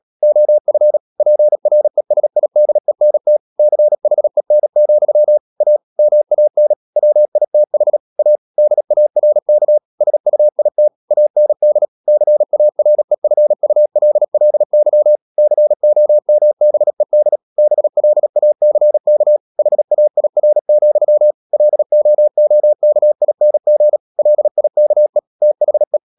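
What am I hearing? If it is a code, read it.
Morse code, heard as a keyed tone, about 37 wpm, 600 Hz. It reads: OF PRESIDENT CHEN, A MAN WITH A DARK SUIT AND CAREFULLY COMBED BLACK HAIR, LOOKING LIKE THE
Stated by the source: Morse code practice at 35–39 wpm sending English text of Ken Follett's novel Never, from CW med Gnister.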